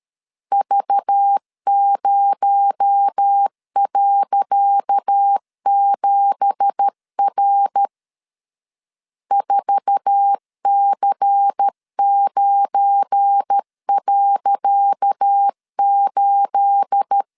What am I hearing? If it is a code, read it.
Morse code, heard as a keyed tone, about 13 wpm, 800 Hz. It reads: V0.7R 4C9.8
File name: LEZIONI / 10 SIGNS / 10C DOT 12wpm 2451